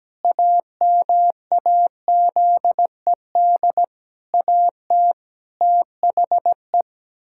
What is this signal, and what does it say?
Morse code, heard as a keyed tone, about 17 wpm, 700 Hz. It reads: AMAZED AT THE
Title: Pangram Challenge v2 - 17wpm